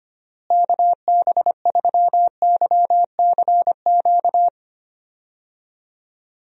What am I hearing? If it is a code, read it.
Morse code, heard as a keyed tone, about 25 wpm, 700 Hz. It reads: KB3YCQ
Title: USA Call Signs - 6 Characters - Repeated 3x - 25wpm